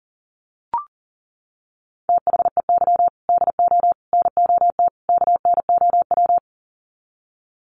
Morse code, heard as a keyed tone, about 40 wpm, 700 Hz. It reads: THEY DO NOT KNOW
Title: Sets of 4 Words 40wpm